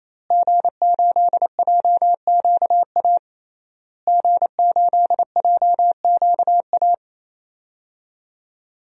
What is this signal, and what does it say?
Morse code, heard as a keyed tone, about 28 wpm, 700 Hz. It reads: G8JQA G8JQA
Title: DXCC-100 Call Signs - 5 Characters - Repeated 3x - 28wpm